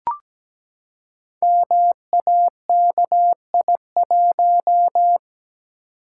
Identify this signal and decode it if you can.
Morse code, heard as a keyed tone, about 17 wpm, 700 Hz. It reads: MAKI1
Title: Code Groups - 5 Letter:Numbers - 17wpm